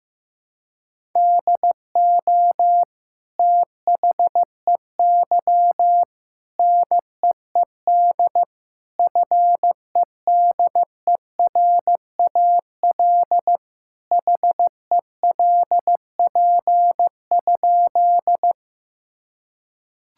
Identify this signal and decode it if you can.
Morse code, heard as a keyed tone, about 15 wpm, 700 Hz. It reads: DO THEY NEED FEDERAL HELP?
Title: Sentences from Top 1000 Words - Review Part 5 - 15wpm